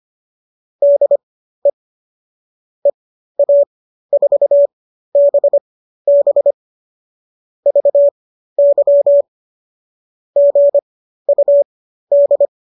Morse code, heard as a keyed tone, about 25 wpm, spaced as for 13 wpm, 570 Hz. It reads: DE EA4BB VY GUD